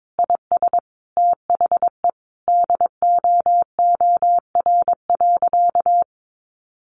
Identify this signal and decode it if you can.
Morse code, heard as a keyed tone, about 22 wpm, 700 Hz. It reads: IS THE DOOR.
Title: Sentences from Top 1100 Words 22wpm